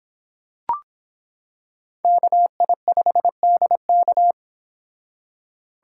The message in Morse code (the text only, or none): KI5DK